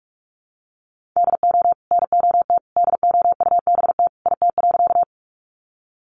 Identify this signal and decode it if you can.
Morse code, heard as a keyed tone, about 44 wpm, 700 Hz. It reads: DO NOT DOUBT IT.
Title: Sentences from Top 1500 Words 45wpm